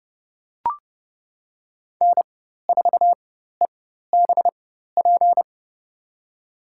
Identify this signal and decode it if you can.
Morse code, heard as a keyed tone, about 30 wpm, spaced as for 14 wpm, 700 Hz. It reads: N4EBP
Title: USA Call Signs - 4x Character Spacing - 30wpm